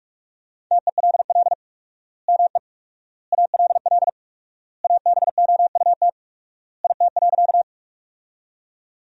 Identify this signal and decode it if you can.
Morse code, heard as a keyed tone, about 47 wpm, 700 Hz. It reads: TELL ME ALL ABOUT IT.
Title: 4x Word-Spacing - Sentences from Top 200 Words - 45wpm